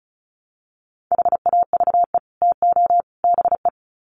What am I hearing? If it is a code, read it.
Morse code, heard as a keyed tone, about 34 wpm, 700 Hz. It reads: HAVE TO BE
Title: Sets of 3 Words 35wpm